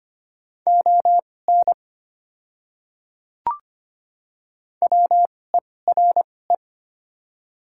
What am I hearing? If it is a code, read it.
Morse code, heard as a keyed tone, about 25 wpm, spaced as for 18 wpm, 700 Hz. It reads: ON WERE
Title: 2x Character Spacing - Top 100 Words - 25wpm